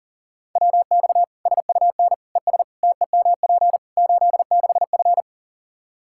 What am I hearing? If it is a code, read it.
Morse code, heard as a keyed tone, about 37 wpm, 700 Hz. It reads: WX SUN ES TEMP 86F
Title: K1USN SST QSO Practice 40wpm